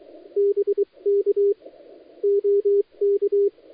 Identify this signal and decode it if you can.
Morse code, heard as a keyed tone, about 23 wpm, spaced as for 20 wpm, 400 Hz. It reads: BK OK
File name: dlazaawansowanych / nagraniazpasma / titawaNETcontrol